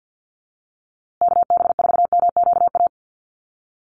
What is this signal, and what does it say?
Morse code, heard as a keyed tone, about 50 wpm, 700 Hz. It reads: KB4MQA
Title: USA Call Signs Repeated 3x 50wpm